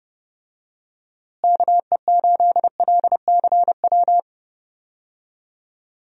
Morse code, heard as a keyed tone, about 30 wpm, 700 Hz. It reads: KE8LCW
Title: USA Call Signs - 6 Characters - Repeated 2x - 30wpm